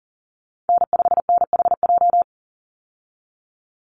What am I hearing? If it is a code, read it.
Morse code, heard as a keyed tone, about 40 wpm, 700 Hz. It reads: N5NHJ
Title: Call Signs Repeated 2x - 40wpm